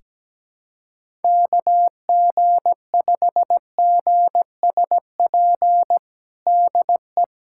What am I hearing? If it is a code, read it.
Morse code, heard as a keyed tone, about 17 wpm, 700 Hz. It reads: KG5GSP DE
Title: USA Callsign DE Callsign 17wpm